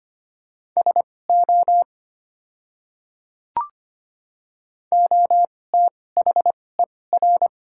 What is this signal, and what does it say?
Morse code, heard as a keyed tone, about 25 wpm, spaced as for 18 wpm, 700 Hz. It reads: SO OTHER
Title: 2x Character Spacing - Top 100 Words - 25wpm